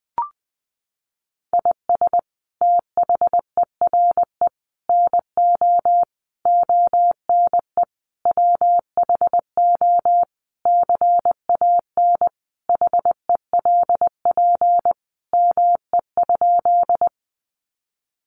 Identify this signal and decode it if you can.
Morse code, heard as a keyed tone, about 20 wpm, 700 Hz. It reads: IS THERE NO ONE WHO CAN HELP ME?
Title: Sentences from Top 200 Words 20wpm